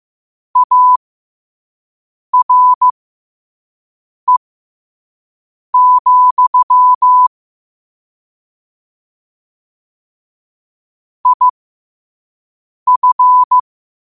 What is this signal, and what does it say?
Morse code, heard as a keyed tone, about 15 wpm, spaced as for 5 wpm, 1000 Hz. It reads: ARE, IF
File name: qotdfiles / 5wpm